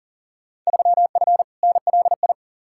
Morse code, heard as a keyed tone, about 40 wpm, 700 Hz. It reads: 3F NLI
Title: Field Day Exchanges Encore - 40wpm